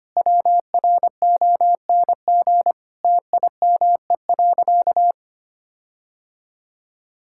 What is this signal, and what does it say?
Morse code, heard as a keyed tone, about 25 wpm, 700 Hz. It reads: WRONG TIME.
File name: practice / 25wpm / Sentences with 9-Word Phrases - 25wpm